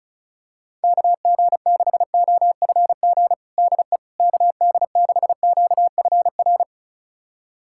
Morse code, heard as a keyed tone, about 36 wpm, 700 Hz. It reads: KG6OFG DE KD6QFR
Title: USA Callsign DE Callsign 35wpm